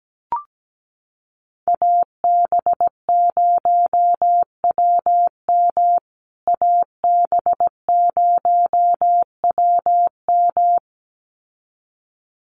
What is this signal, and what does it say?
Morse code, heard as a keyed tone, about 17 wpm, 700 Hz. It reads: AB0WM AB0WM
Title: USA Call Signs Repeated 2x 17wpm